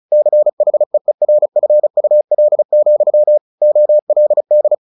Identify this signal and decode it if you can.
Morse code, heard as a keyed tone, about 34 wpm, 600 Hz. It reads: CHEERFUL, OLD
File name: the old man and the sea 2 35wpm